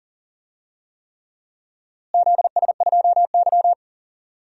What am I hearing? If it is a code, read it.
Morse code, heard as a keyed tone, about 42 wpm, 700 Hz. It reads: ZS2Y
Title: DXCC-100 Call Signs - 4 Characters - Repeated 3x - 40wpm